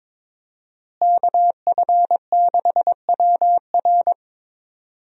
Morse code, heard as a keyed tone, about 22 wpm, 700 Hz. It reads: KF6WR
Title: USA Call Signs - 5 Characters - Repeated 2x - 22wpm